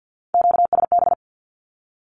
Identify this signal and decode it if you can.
Morse code, heard as a keyed tone, about 48 wpm, 700 Hz. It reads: QSB
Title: Q-Codes 50wpm